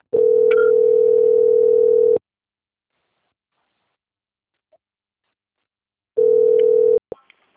Ringback tone (North America), 440 + 480 Hz.
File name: out- - - - - .11092